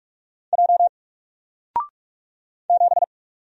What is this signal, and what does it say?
Morse code, heard as a keyed tone, about 44 wpm, 700 Hz. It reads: J 7
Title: Single Character (Letter:Number:Punctuation:Prosign) Alt 45wpm